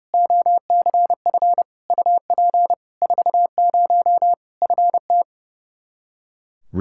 Morse code, heard as a keyed tone, about 30 wpm, 700 Hz. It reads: OCF UP 40 FT
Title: Practice QSO - Exchange 2 - Info - 30wpm